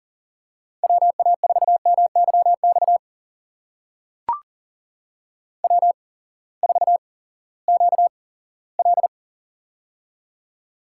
Morse code, heard as a keyed tone, about 40 wpm, spaced as for 20 wpm, 700 Hz. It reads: WA4MYX W4QL
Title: USA Call Signs - 8x Character Spacing - 40wpm